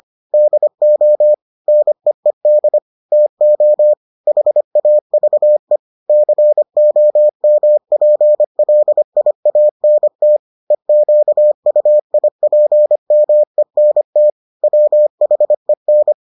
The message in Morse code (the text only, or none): DO NEED TO HAVE COMPLIANT EQUIPMENT WHEN